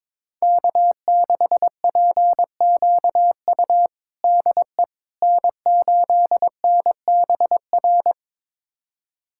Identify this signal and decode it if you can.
Morse code, heard as a keyed tone, about 22 wpm, 700 Hz. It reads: K6PQU DE N8NBR
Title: USA Callsign DE Callsign - 5 Characters - 22wpm